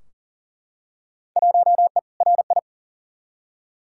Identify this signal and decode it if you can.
Morse code, heard as a keyed tone, about 40 wpm, 700 Hz. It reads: 1E RI
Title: Field Day Exchanges Encore - 40wpm